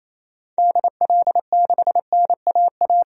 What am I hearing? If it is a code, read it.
Morse code, heard as a keyed tone, about 28 wpm, 700 Hz. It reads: DL6NAA
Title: DXCC-100 Call Signs - 6 Characters - Repeated 3x - 28wpm